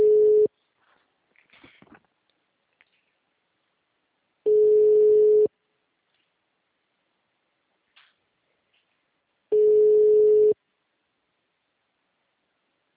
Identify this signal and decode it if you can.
Ringback tone (Europe, 425 Hz).